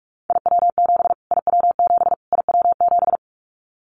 Morse code, heard as a keyed tone, about 44 wpm, 700 Hz. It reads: IW7 IW7 IW7
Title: Callsign Prefixes for Top 25 DXCC - Repeated 3x - 45wpm